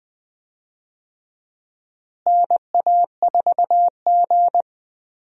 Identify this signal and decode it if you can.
Morse code, heard as a keyed tone, about 20 wpm, 700 Hz. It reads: NA4G
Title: USA Call Signs - 4 Characters - Repeated 3x - 20wpm